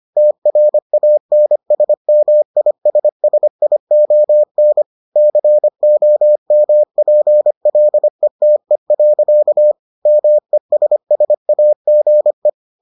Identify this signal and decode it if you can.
Morse code, heard as a keyed tone, about 25 wpm, 600 Hz. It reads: TRANSMISSION COMPLETE. MESSAGE